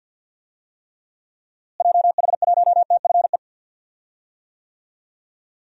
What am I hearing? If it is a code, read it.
Morse code, heard as a keyed tone, about 49 wpm, 700 Hz. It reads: JH1TFE
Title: DXCC-100 Call Signs - 6 Characters - Repeated 2x - 50wpm